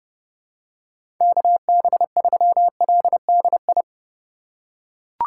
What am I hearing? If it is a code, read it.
Morse code, heard as a keyed tone, about 32 wpm, 700 Hz. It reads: KB3LDI
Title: USA Call Signs - 6 Characters - 4x Character Spacing - 30wpm